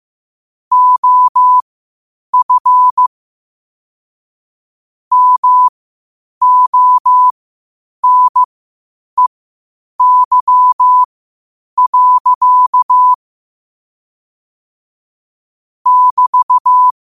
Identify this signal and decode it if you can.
Morse code, heard as a keyed tone, about 15 wpm, spaced as for 8 wpm, 1000 Hz. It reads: OF MONEY. <BT>
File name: qotdfiles / 8wpm